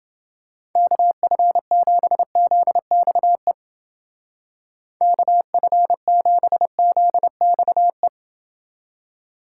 Morse code, heard as a keyed tone, about 29 wpm, 700 Hz. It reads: KF7ZXE KF7ZXE